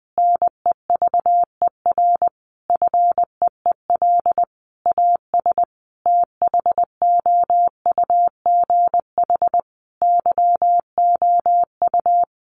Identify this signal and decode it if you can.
Morse code, heard as a keyed tone, about 20 wpm, 700 Hz. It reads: NEVER FEEL AS THOUGH YOU
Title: Sentences from Top 1500 Words 20wpm